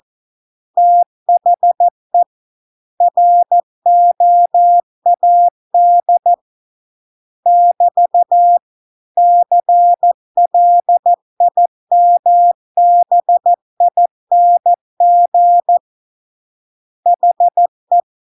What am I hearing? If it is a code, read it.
Morse code, heard as a keyed tone, about 14 wpm, 700 Hz. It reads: THE ROAD <BT> CLIMBING HE